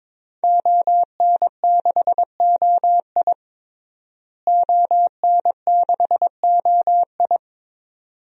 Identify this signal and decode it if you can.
Morse code, heard as a keyed tone, about 22 wpm, 700 Hz. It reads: ON6OI ON6OI
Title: DXCC-100 Call Signs - 5 Characters - Repeated 2x - 22wpm